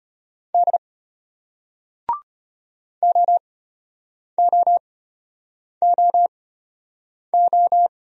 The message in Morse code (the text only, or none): D OOOO